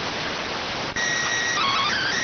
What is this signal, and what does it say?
SSTV (slow-scan television), PD 290.